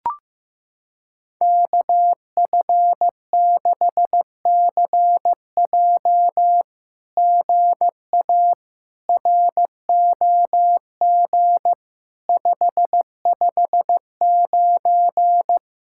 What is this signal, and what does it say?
Morse code, heard as a keyed tone, about 15 wpm, 700 Hz. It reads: KF6CJ GA ROG 559
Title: SOTA QSO Practice 15wpm